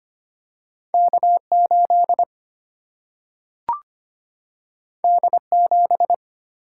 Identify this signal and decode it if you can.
Morse code, heard as a keyed tone, about 25 wpm, 700 Hz. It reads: K8 D7
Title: Single-Letter Single-Number 25wpm